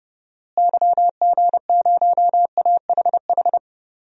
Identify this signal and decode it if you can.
Morse code, heard as a keyed tone, about 30 wpm, 700 Hz. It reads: YG0AHH
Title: DXCC-100 Call Signs - 6 Characters - Repeated 3x - 30wpm